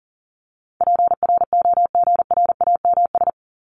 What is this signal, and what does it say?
Morse code, heard as a keyed tone, about 40 wpm, 700 Hz. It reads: PROGRAMS
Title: ES:S Suffix - Top 100 Words with Base Words 40wpm